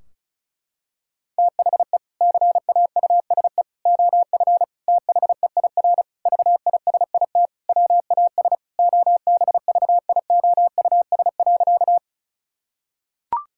Morse code, heard as a keyed tone, about 36 wpm, 700 Hz. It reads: THE CAUSE OF THEIR VISIT WAS OBVIOUS.
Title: Sentences from Top 1600 Words 35wpm